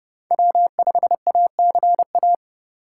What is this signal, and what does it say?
Morse code, heard as a keyed tone, about 30 wpm, 700 Hz. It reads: W5ACA
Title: USA Call Signs - 5 Characters - 6x Character Spacing - 30wpm